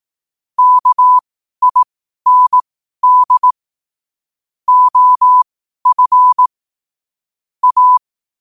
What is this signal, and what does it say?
Morse code, heard as a keyed tone, about 18 wpm, spaced as for 12 wpm, 1000 Hz. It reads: KIND OF A
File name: qotdfiles / 12wpm